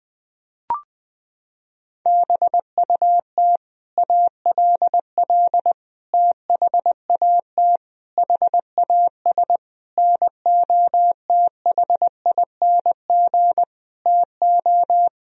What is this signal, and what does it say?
Morse code, heard as a keyed tone, about 20 wpm, 700 Hz. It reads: BUT ALL THAT HAS NOTHING TO
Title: Sentences from Top 1000 Words - Review Part 3 - 20wpm